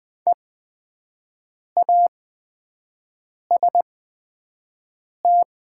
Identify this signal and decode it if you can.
Morse code, heard as a keyed tone, about 20 wpm, spaced as for 5 wpm, 700 Hz. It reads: EAST
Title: 8x Character Spacing - Top 500 Words - 20wpm